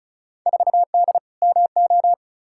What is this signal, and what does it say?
Morse code, heard as a keyed tone, about 36 wpm, 700 Hz. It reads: 4D MO
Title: Field Day Exchanges Encore - 35wpm